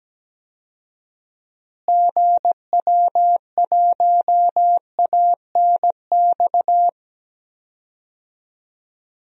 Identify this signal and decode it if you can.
Morse code, heard as a keyed tone, about 17 wpm, 700 Hz. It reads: GW1ANX
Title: DXCC-100 Call Signs - 6 Characters - Repeated 3x - 17wpm